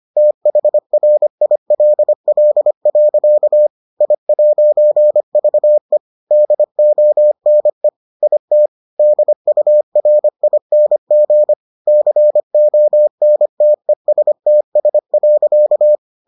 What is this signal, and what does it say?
Morse code, heard as a keyed tone, about 25 wpm, 600 Hz. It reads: THRILL. I'VE DONE IT DURING CONTESTS.